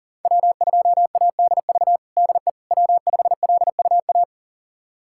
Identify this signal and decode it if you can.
Morse code, heard as a keyed tone, about 40 wpm, 700 Hz. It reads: W2ADV DE W5LUA
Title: USA Callsign DE Callsign - 5 Characters - 40wpm